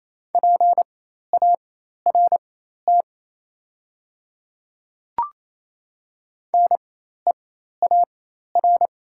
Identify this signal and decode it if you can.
Morse code, heard as a keyed tone, about 29 wpm, spaced as for 13 wpm, 700 Hz. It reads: PART NEAR
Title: 4x Character Spacing - Top 300 Words - 28wpm